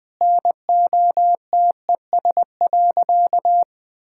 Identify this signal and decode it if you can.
Morse code, heard as a keyed tone, about 20 wpm, 700 Hz. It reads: NOTES.
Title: Easy Sentences with Hints - 20wpm